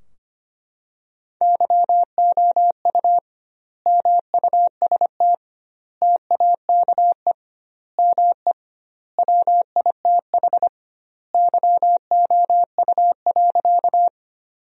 Morse code, heard as a keyed tone, about 25 wpm, 700 Hz. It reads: YOU MUST TAKE ME WITH YOU.
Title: 2x Word-Spacing - Sentences from Top 200 Words - 25wpm